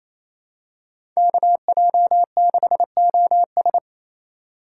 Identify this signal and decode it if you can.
Morse code, heard as a keyed tone, about 27 wpm, 700 Hz. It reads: KJ6OS
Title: USA Call Signs - 5 Characters - 4x Character Spacing - 28wpm